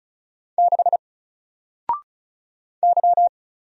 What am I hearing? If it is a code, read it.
Morse code, heard as a keyed tone, about 35 wpm, 700 Hz. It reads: 6 Y
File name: Single-Letter-Number-35wpm